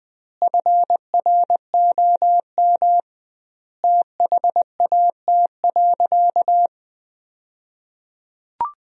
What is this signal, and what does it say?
Morse code, heard as a keyed tone, about 20 wpm, 700 Hz. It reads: FROM THAT.
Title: 2x Word-Spacing - Sentences from Top 300 Words - 20wpm